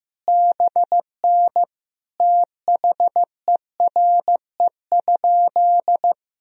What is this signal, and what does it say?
Morse code, heard as a keyed tone, about 15 wpm, 700 Hz. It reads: BN THERE?
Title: Sentences from Top 300 Words and Abbreviations 15wpm